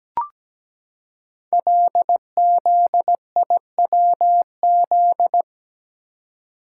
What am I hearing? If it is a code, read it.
Morse code, heard as a keyed tone, about 17 wpm, 700 Hz. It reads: LZIWZ
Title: Code Groups - 5 Letters - 17wpm